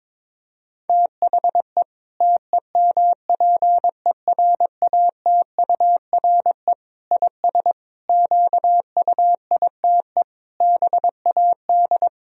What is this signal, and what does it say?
Morse code, heard as a keyed tone, about 22 wpm, 700 Hz. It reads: THE TEMPERATURE IS QUITE BAD
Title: Sentences from Top 1000 Words - Review Part 4 - 22wpm